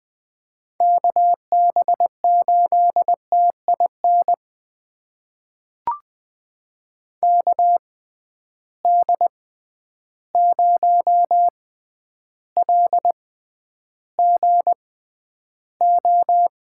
Morse code, heard as a keyed tone, about 20 wpm, spaced as for 10 wpm, 700 Hz. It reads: KB8TIN KD0LGO
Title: USA Call Signs - 6 Characters - 6x Character Spacing - 20wpm